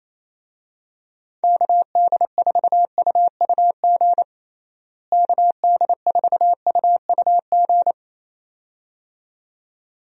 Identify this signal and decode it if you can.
Morse code, heard as a keyed tone, about 28 wpm, 700 Hz. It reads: KD4UUG KD4UUG
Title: USA Call Signs - 6 Characters - Repeated 2x - 28wpm